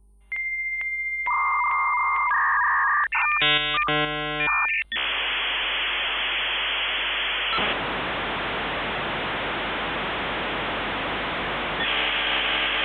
Dial-up modem handshake.